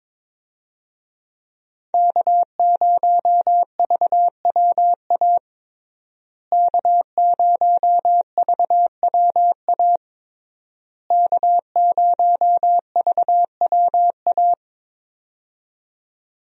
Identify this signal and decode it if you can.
Morse code, heard as a keyed tone, about 22 wpm, 700 Hz. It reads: K0VWA K0VWA K0VWA
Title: USA Call Signs - 5 Characters - Repeated 3x - 22wpm